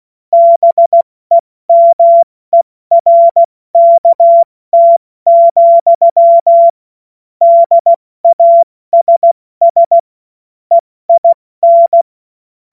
Morse code, heard as a keyed tone, about 16 wpm, 680 Hz. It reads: BEMERKT, DASS EIN
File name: Morsekurs / Witze / Witz 06. Dezember 2022 70 F80